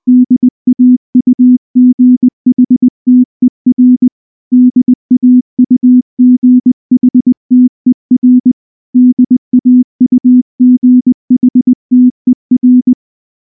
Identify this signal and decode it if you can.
Morse code, heard as a keyed tone, about 19 wpm, 250 Hz. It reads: DAUGHTER DAUGHTER DAUGHTER
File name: morse / morse-training-20WPM-2